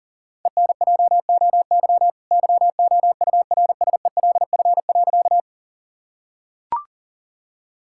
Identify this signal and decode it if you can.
Morse code, heard as a keyed tone, about 37 wpm, 700 Hz. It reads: ENJOY YOURSELF.
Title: Sentences from Top 1000 Words - Review Part 5 - 40wpm